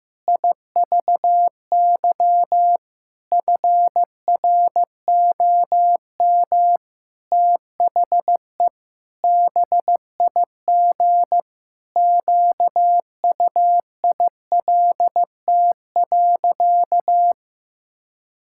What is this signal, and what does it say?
Morse code, heard as a keyed tone, about 15 wpm, 700 Hz. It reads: IVY FROM THE BIG QUILT.
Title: Pangram Challenge v2 - 15wpm